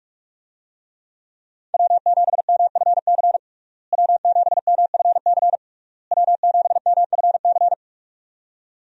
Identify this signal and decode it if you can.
Morse code, heard as a keyed tone, about 46 wpm, 700 Hz. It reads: W7MFC W7MFC W7MFC
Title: USA Call Signs - 5 Characters - Repeated 3x - 45wpm